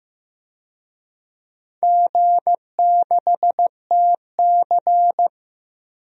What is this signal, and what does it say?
Morse code, heard as a keyed tone, about 15 wpm, 700 Hz. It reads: G6TC
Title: DXCC-100 Call Signs - 4 Characters - 15wpm